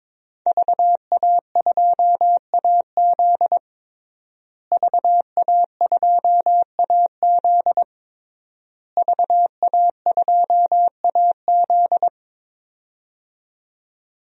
Morse code, heard as a keyed tone, about 22 wpm, 700 Hz. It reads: VA2AZ VA2AZ VA2AZ
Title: DXCC-100 Call Signs - 5 Characters - Repeated 3x - 22wpm